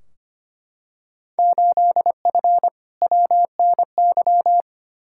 Morse code, heard as a keyed tone, about 25 wpm, 700 Hz. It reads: 8F WNY
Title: Field Day Exchanges Encore - 25wpm